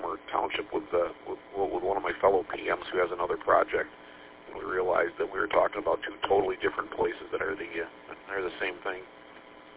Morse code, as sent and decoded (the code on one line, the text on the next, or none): none